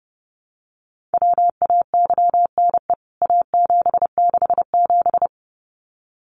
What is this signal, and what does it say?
Morse code, heard as a keyed tone, about 30 wpm, 700 Hz. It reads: WAYNE A767
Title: CWT Exchange 30wpm